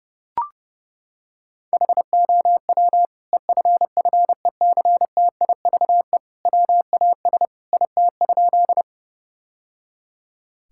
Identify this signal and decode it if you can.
Morse code, heard as a keyed tone, about 30 wpm, 700 Hz. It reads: HOW EFFECTIVE WAS IT?
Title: Sentences from Top 1000 Words - Review Part 6 - 30wpm